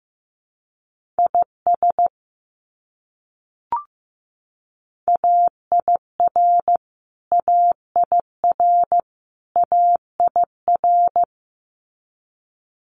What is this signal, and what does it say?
Morse code, heard as a keyed tone, about 15 wpm, 700 Hz. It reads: IS AIR AIR AIR
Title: Top 300 Words Repeated 15wpm